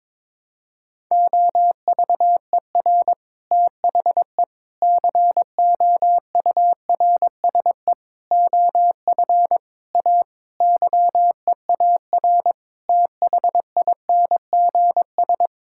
Morse code, heard as a keyed tone, about 22 wpm, 700 Hz. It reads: OVER THE COURSE OF A YEAR THINGS
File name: practice / 22wpm / Sentences with 4-Word Phrases - 22wpm